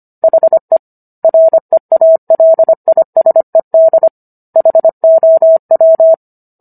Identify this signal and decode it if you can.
Morse code, heard as a keyed tone, about 25 wpm, 650 Hz. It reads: HE REALISED HOW